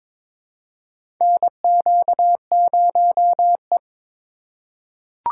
Morse code, heard as a keyed tone, about 21 wpm, 700 Hz. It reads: NQ0E
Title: USA Call Signs - 4 Characters - 2x Character Spacing - 22wpm